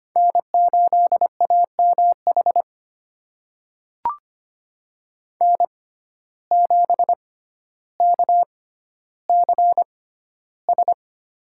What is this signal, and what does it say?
Morse code, heard as a keyed tone, about 25 wpm, spaced as for 13 wpm, 700 Hz. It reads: N8AMH N7KCS
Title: USA Call Signs - 5 Characters - 6x Character Spacing - 25wpm